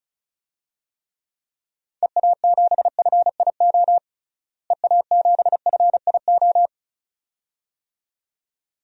Morse code, heard as a keyed tone, about 36 wpm, 700 Hz. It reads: EA7FIO EA7FIO
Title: DXCC-100 Call Signs - 6 Characters - Repeated 2x - 35wpm